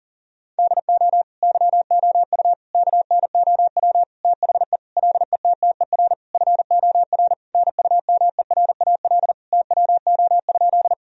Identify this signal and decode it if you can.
Morse code, heard as a keyed tone, about 40 wpm, 700 Hz. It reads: DO YOU KNOW THE LETTER FOR NUMERAL TWO?
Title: Sentences from Top 1000 Words - Review Part 2 - 40wpm